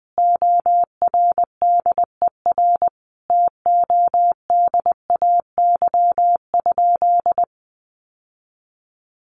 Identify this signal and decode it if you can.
Morse code, heard as a keyed tone, about 20 wpm, 700 Hz. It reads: ORDER TODAY?
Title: Sentences from Top 1500 Words 20wpm